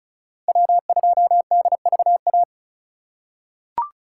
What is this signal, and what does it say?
Morse code, heard as a keyed tone, about 36 wpm, 690 Hz. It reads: W2DVA
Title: USA Call Signs - 5 Characters - 2x Character Spacing - 35wpm